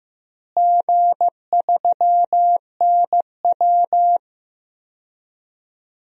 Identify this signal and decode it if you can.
Morse code, heard as a keyed tone, about 15 wpm, 700 Hz. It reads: G3NW
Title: DXCC-100 Call Signs - 4 Characters - Repeated 3x - 15wpm